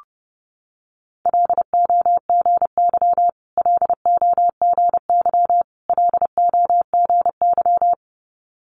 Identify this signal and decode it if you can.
Morse code, heard as a keyed tone, about 30 wpm, 700 Hz. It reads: LOGY LOGY LOGY
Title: Multi-letter Phonemes Repeated 3x 30wpm